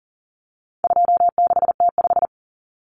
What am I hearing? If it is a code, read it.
Morse code, heard as a keyed tone, about 40 wpm, 700 Hz. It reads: 26T5
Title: Common Cut Numbers - 4-Digit 40wpm